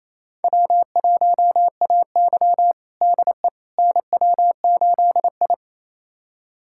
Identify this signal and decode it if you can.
Morse code, heard as a keyed tone, about 28 wpm, 700 Hz. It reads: W1AY DE NW8I
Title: USA Callsign DE Callsign - 4 Characters - 28wpm